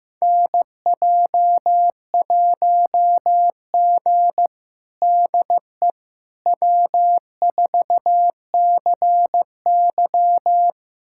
Morse code, heard as a keyed tone, about 16 wpm, 700 Hz. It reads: NJ1G DE W4CY
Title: USA Callsign DE Callsign - 4 Characters - 15wpm